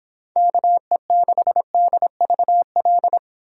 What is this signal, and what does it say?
Morse code, heard as a keyed tone, about 25 wpm, 700 Hz. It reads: KE6DVL